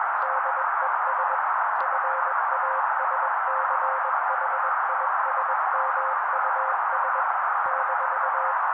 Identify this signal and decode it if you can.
Morse code, heard as a keyed tone, about 21 wpm, 530 Hz. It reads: DES FASCHISMUS -
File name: daily / 20wpm noise